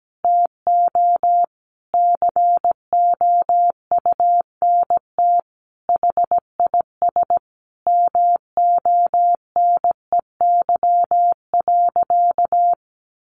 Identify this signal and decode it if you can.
Morse code, heard as a keyed tone, about 17 wpm, 700 Hz. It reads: TO COUNT HIS MONEY.
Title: Sentences from Top 1500 Words 17wpm